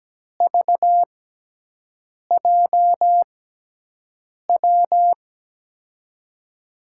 Morse code, heard as a keyed tone, about 17 wpm, spaced as for 6 wpm, 700 Hz. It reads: VJW
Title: USA Call Signs - 6 Characters - 6x Character Spacing - 17wpm